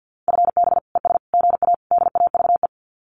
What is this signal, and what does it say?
Morse code, heard as a keyed tone, about 50 wpm, 700 Hz. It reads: FB ES GA DAVE
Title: Common QSO Phrases Encore 50wpm